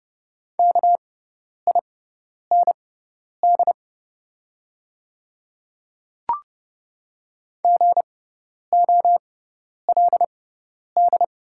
Morse code, heard as a keyed tone, about 31 wpm, spaced as for 10 wpm, 700 Hz. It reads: KIND GOLD
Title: 6x Character Spacing - Top 500 Words - 30wpm